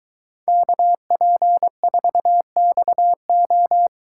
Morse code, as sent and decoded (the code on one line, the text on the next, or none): -.- .--. ....- -..- ---
KP4XO